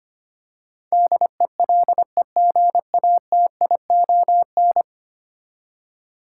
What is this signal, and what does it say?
Morse code, heard as a keyed tone, about 25 wpm, 700 Hz. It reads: DELEGATION
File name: Ten Letter Words - 25wpm